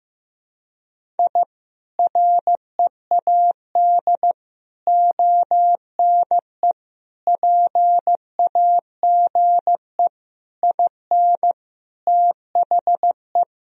Morse code, heard as a keyed tone, about 15 wpm, 700 Hz. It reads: I READ ONE PAGE IN THE
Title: Sentences from Top 1000 Words - Review Part 1 - 15wpm